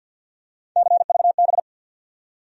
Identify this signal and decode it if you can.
Morse code, heard as a keyed tone, about 54 wpm, 700 Hz. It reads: CVB
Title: USA Call Sign Postfixes - 3 Letters 50wpm